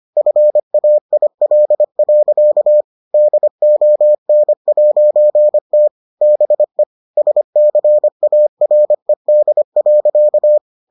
Morse code, heard as a keyed tone, about 25 wpm, 600 Hz. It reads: FAIL. DON'T BE SCARED.